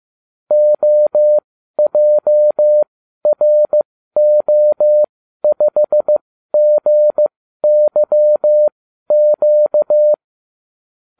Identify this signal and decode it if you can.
Morse code, heard as a keyed tone, about 15 wpm, spaced as for 12 wpm, 600 Hz. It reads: OJRO5GYQ